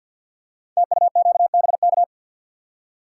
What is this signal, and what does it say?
Morse code, heard as a keyed tone, about 47 wpm, 700 Hz. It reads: TUQBK
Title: Code Groups - 5 Letters - 50wpm